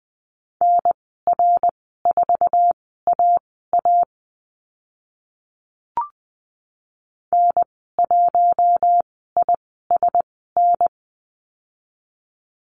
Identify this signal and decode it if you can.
Morse code, heard as a keyed tone, about 20 wpm, spaced as for 14 wpm, 700 Hz. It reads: NR4AA N1ISN
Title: 062 Lesson 11g - Letter h - Call Signs 20wpm